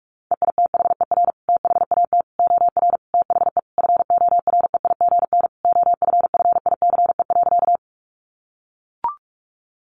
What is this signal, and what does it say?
Morse code, heard as a keyed tone, about 44 wpm, 700 Hz. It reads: EITHER THAT OR THE FOREIGN OFFICE.
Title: Sentences from Top 1400 Words 45wpm